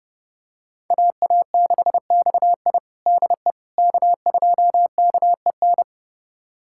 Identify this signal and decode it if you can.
Morse code, heard as a keyed tone, about 32 wpm, 700 Hz. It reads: AA6XI DE K2KEN
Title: USA Callsign DE Callsign - 5 Characters - 30wpm